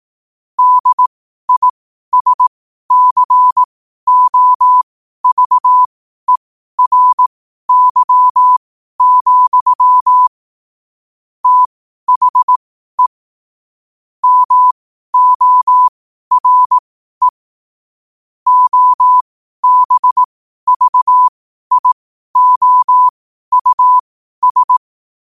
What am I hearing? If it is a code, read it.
Morse code, heard as a keyed tone, about 18 wpm, spaced as for 12 wpm, 1000 Hz. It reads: DISCOVERY, THE MORE OBVIOUS
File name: 12wpm